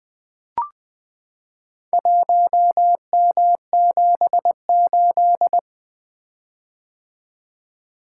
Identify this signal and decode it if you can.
Morse code, heard as a keyed tone, about 20 wpm, 700 Hz. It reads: 1M78
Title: Four Letter-Number Combinations - 20wpm